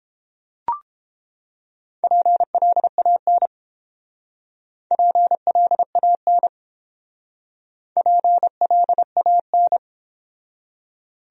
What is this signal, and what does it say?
Morse code, heard as a keyed tone, about 29 wpm, 700 Hz. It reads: PLAN PLAN PLAN